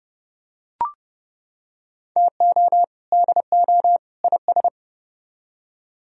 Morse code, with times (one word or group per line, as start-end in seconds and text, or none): 2.2-2.8 TO
3.1-4.0 DO
4.2-4.7 IS